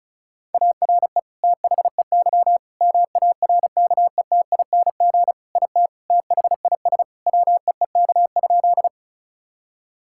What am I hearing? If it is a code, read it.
Morse code, heard as a keyed tone, about 36 wpm, 700 Hz. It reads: ARE THEY MARKETING IT THIS WEEK?
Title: Sentences from Top 1000 Words - Review Part 5 - 35wpm